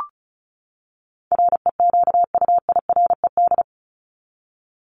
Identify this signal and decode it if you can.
Morse code, heard as a keyed tone, about 34 wpm, 700 Hz. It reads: REQUIRED
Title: ED Suffix - Top 100 Words with Base Words 35wpm